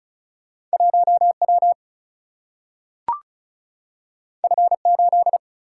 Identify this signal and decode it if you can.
Morse code, heard as a keyed tone, about 36 wpm, 700 Hz. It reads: 1W F8
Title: Single-Letter Single-Number 35wpm